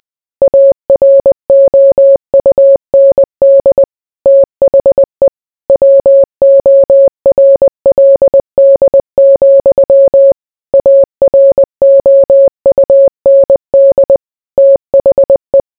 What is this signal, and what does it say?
Morse code, heard as a keyed tone, about 20 wpm, 550 Hz. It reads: AROUND THE WORLD, AROUND THE